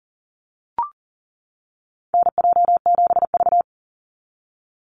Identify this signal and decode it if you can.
Morse code, heard as a keyed tone, about 40 wpm, 700 Hz. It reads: NJ7V
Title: NA SOTA Activator Callsigns 40wpm